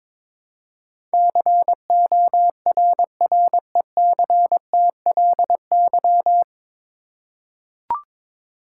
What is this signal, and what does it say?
Morse code, heard as a keyed tone, about 22 wpm, 700 Hz. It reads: CORRECTLY
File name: 9 Letter Words - First 3 Letters Given - 22wpm